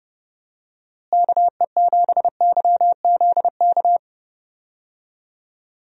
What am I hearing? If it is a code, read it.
Morse code, heard as a keyed tone, about 31 wpm, 700 Hz. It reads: KE7YZK
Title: USA Call Signs - 6 Characters - Repeated 2x - 30wpm